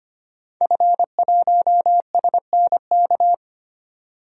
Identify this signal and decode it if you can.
Morse code, heard as a keyed tone, about 25 wpm, 700 Hz. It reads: F1SNK
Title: DXCC-100 Call Signs - 5 Characters - 25wpm